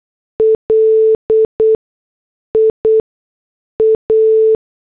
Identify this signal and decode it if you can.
Morse code, heard as a keyed tone, about 8 wpm, 440 Hz. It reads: L I A
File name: morse / headlines